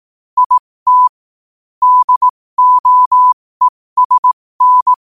Morse code, heard as a keyed tone, about 18 wpm, spaced as for 15 wpm, 1000 Hz. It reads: IT DOESN
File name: qotdfiles / 15wpm